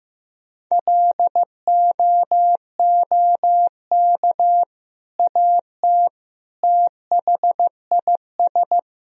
Morse code, heard as a keyed tone, about 16 wpm, 700 Hz. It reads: LOOK AT THIS
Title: Sentences from Top 1000 Words - Review Part 2 - 15wpm